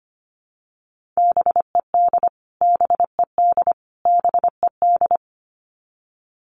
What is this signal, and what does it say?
Morse code, heard as a keyed tone, about 25 wpm, 700 Hz. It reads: BED BED BED
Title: Top 500 Words Repeated 25wpm